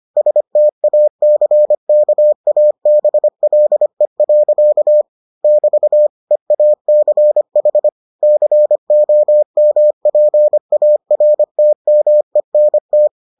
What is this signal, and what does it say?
Morse code, heard as a keyed tone, about 25 wpm, 600 Hz. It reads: STACKABLE. <BT> EACH COMPARTMENT